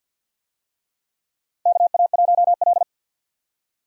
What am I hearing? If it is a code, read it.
Morse code, heard as a keyed tone, about 52 wpm, 700 Hz. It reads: KA1L
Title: USA Call Signs - 4 Characters - Repeated 2x - 50wpm